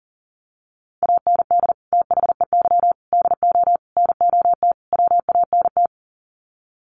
Morse code, heard as a keyed tone, about 40 wpm, 700 Hz. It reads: AND THEY DO NOT WANT
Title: Sets of 5 Words Encore2 40wpm